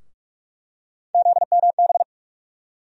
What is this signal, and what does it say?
Morse code, heard as a keyed tone, about 43 wpm, 700 Hz. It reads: ZMB
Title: USA Call Sign Postfixes - 3 Letters 45wpm